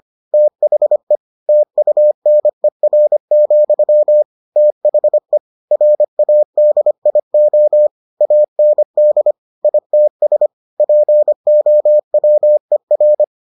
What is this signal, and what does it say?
Morse code, heard as a keyed tone, about 25 wpm, 600 Hz. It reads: THE TUNER, THE RADIO AND ITS POWER